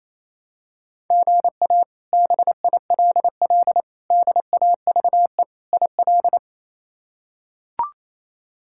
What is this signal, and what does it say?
Morse code, heard as a keyed tone, about 28 wpm, 700 Hz. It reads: GA BILL DAVE IL
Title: K1USN SST QSO Practice - 28wpm